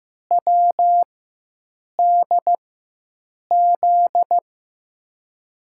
Morse code, heard as a keyed tone, about 15 wpm, spaced as for 7 wpm, 700 Hz. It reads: WDZ
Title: USA Call Signs - 6 Characters - 4x Character Spacing - 15wpm